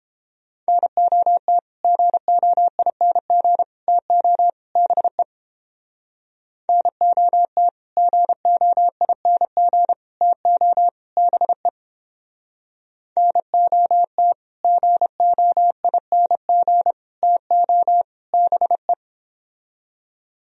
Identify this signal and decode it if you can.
Morse code, heard as a keyed tone, about 30 wpm, 700 Hz. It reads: NOT GOING TO BE NOT GOING TO BE NOT GOING TO BE